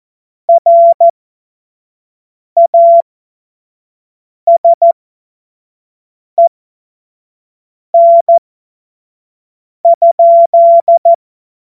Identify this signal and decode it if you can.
Morse code, heard as a keyed tone, about 14 wpm, spaced as for 5 wpm, 680 Hz. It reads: RASEN?
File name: Morsekurs / Witze / Witz 31. Mai 2022 25 F70